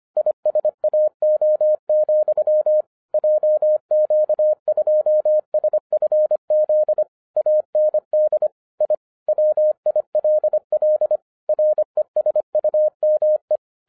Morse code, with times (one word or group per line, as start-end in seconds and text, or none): 0.2-2.8 ISAO,
3.1-7.0 JQ2SFZ
7.4-8.5 AND
8.8-8.9 I
9.3-11.2 WILL
11.5-13.6 RESUME